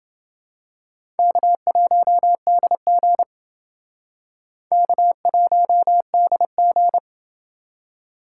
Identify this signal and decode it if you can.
Morse code, heard as a keyed tone, about 29 wpm, 700 Hz. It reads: K1DG K1DG